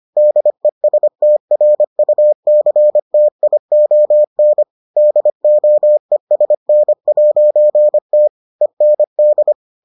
Morse code, heard as a keyed tone, about 25 wpm, 600 Hz. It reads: DESTRUCTION DOESN'T END